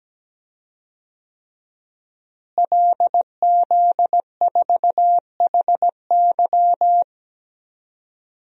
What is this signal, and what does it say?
Morse code, heard as a keyed tone, about 17 wpm, 700 Hz. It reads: LZ4HY
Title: DXCC-100 Call Signs - 5 Characters - Repeated 3x - 17wpm